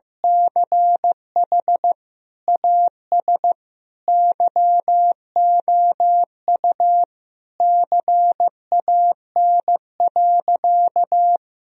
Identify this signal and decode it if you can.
Morse code, heard as a keyed tone, about 15 wpm, 700 Hz. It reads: CH AS YOU CAN.
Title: Sentences from Top 1000 Words - Review Part 1 - 15wpm